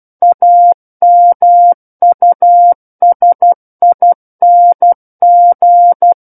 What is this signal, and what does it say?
Morse code, heard as a keyed tone, about 12 wpm, 700 Hz. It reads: AMUSING
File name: Book One / 03-Chapter 3 - 12 WPM